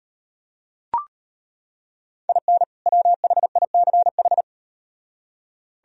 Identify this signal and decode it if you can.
Morse code, heard as a keyed tone, about 38 wpm, 700 Hz. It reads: IN WHICH